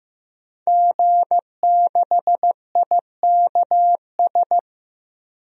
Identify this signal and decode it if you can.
Morse code, heard as a keyed tone, about 15 wpm, 700 Hz. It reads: G6IKS
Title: DXCC-100 Call Signs - 5 Characters - Repeated 2x - 15wpm